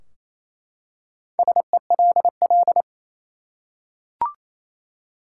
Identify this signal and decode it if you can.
Morse code, heard as a keyed tone, about 28 wpm, 700 Hz. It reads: SELL
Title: Four Letter Words - 28wpm